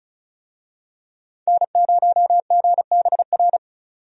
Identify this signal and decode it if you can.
Morse code, heard as a keyed tone, about 36 wpm, 700 Hz. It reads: N0GBR
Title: USA Call Signs - 5 Characters - 35wpm